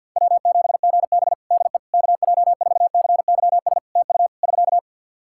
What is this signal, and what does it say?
Morse code, heard as a keyed tone, about 50 wpm, 700 Hz. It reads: W7GB DE KJ4CYS TU <SK>
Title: Practice QSO - Exchange 4 - Ending - 50wpm